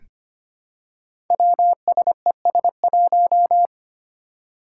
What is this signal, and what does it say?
Morse code, heard as a keyed tone, about 25 wpm, 700 Hz. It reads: WSES1
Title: Five Challenging Letter-Number Combinations 25wpm